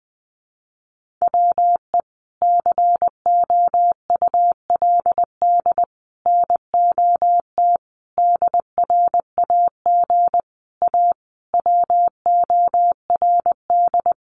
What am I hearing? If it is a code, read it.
Morse code, heard as a keyed tone, about 20 wpm, 700 Hz. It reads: WE COULD NOT DRAG A WORD
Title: Sentences from Top 1500 Words 20wpm